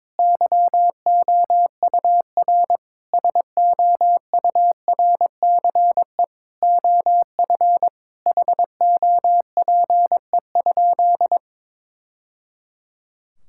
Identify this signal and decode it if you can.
Morse code, heard as a keyed tone, about 22 wpm, 700 Hz. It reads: YOUR SOURCE OF HOPE?
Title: Sentences from Top 1000 Words - Review Part 5 - 22wpm